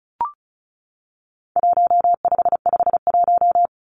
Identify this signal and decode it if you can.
Morse code, heard as a keyed tone, about 34 wpm, 700 Hz. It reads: 1551
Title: Numbers - Contest Style - 4-Digit 35wpm